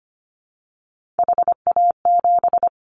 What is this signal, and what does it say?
Morse code, heard as a keyed tone, about 25 wpm, 700 Hz. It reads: HA7
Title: Callsign Prefixes for Top 50 DXCC - 25wpm